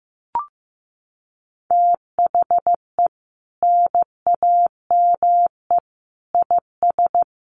Morse code, heard as a keyed tone, about 14 wpm, 700 Hz. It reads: THE NAME IS
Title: Sentences from Top 1000 Words 15wpm